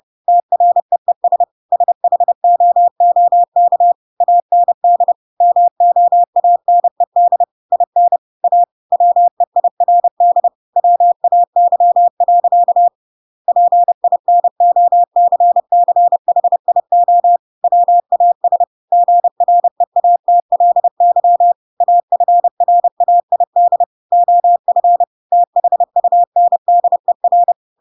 Morse code, heard as a keyed tone, about 30 wpm, 700 Hz. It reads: TREES SHOOK AND MOANED IN A WEIRD WAY. PINOCCHIO WAS GREATLY AFRAID OF THUNDER
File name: Pinocchio / Pinocchio ch6 30 30 ik1hge